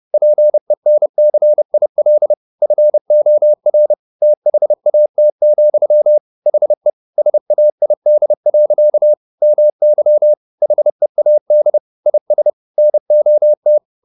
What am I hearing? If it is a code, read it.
Morse code, heard as a keyed tone, about 30 wpm, 600 Hz. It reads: PENCIL FOR THAT, HE SAID. MY HEAD IS NOT